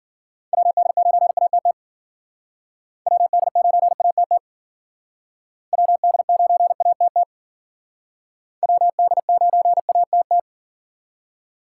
Morse code, heard as a keyed tone, about 53 wpm, 700 Hz. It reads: WD9ATT WD9ATT WD9ATT WD9ATT